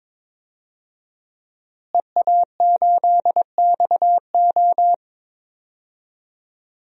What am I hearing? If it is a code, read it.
Morse code, heard as a keyed tone, about 22 wpm, 700 Hz. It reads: EA8XO
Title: DXCC-100 Call Signs - 5 Characters - Repeated 3x - 22wpm